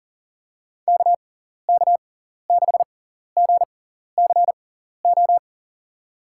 Morse code, heard as a keyed tone, about 40 wpm, spaced as for 14 wpm, 700 Hz. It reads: KK6GCO
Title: USA Call Signs - 6x Character Spacing - 40wpm